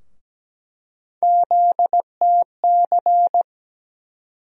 Morse code, heard as a keyed tone, about 17 wpm, 700 Hz. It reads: ZTC
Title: USA Call Sign Postfixes - 3 Letters 17wpm